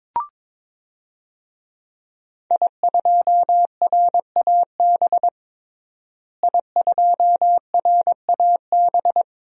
Morse code, heard as a keyed tone, about 22 wpm, 700 Hz. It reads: I2RAB I2RAB
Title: DXCC-100 Call Signs - 5 Characters - Repeated 3x - 22wpm